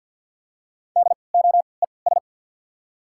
Morse code, heard as a keyed tone, about 54 wpm, spaced as for 29 wpm, 700 Hz. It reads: DOES
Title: 3x Character Spacing - Top 200 Words - 50wpm